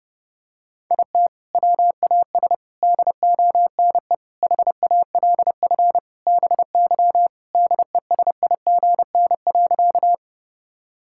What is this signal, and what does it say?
Morse code, heard as a keyed tone, about 32 wpm, 700 Hz. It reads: IT WAS DONE HALF BY DESIGN.
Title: Sentences from Top 1000 Words - Review Part 5 - 30wpm